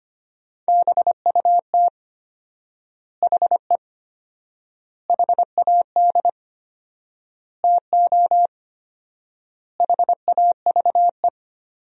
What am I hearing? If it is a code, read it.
Morse code, heard as a keyed tone, about 25 wpm, 700 Hz. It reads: BUT HE HAD TO HAVE
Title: 4x Word-Spacing - Sentences from Top 100 Words - 25wpm